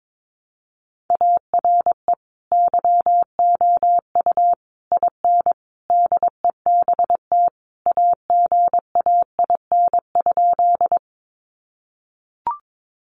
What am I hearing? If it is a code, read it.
Morse code, heard as a keyed tone, about 22 wpm, 700 Hz. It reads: ARE YOU IN DEBT AGAIN?
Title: Sentences from Top 1300 Words 22wpm